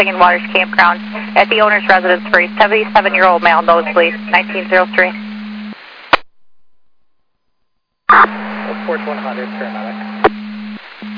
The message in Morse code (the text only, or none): none